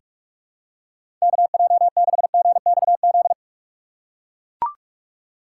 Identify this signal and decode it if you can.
Morse code, heard as a keyed tone, about 44 wpm, 700 Hz. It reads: KJ6GXZ
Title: USA Call Signs - 6 Characters - 2x Character Spacing - 45wpm